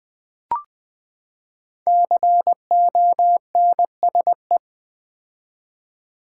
Morse code, heard as a keyed tone, about 19 wpm, 700 Hz. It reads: CONSE
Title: Build A Word - 5 Letters At A Time - 12-Letter Words - 20wpm